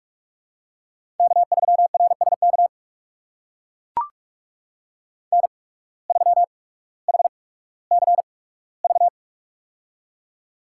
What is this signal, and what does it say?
Morse code, heard as a keyed tone, about 44 wpm, spaced as for 19 wpm, 700 Hz. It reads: K3RSK N3HCV
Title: USA Call Signs - 5 Characters - 8x Character Spacing - 45wpm